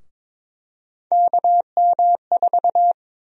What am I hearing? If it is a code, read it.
Morse code, heard as a keyed tone, about 22 wpm, 700 Hz. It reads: KM4
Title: USA Call Sign Prefixes - 22wpm